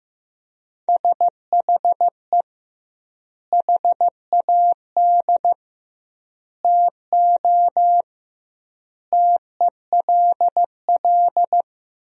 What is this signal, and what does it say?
Morse code, heard as a keyed tone, about 15 wpm, 700 Hz. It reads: SHE HAD TO TELL
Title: 2x Word-Spacing - Sentences from Top 200 Words - 15wpm